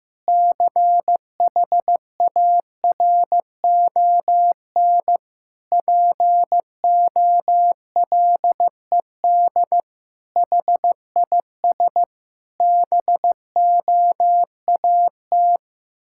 Morse code, heard as a keyed tone, about 15 wpm, 700 Hz. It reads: CHARON POLED HIS BOAT
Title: Pangram Challenge v2 - 15wpm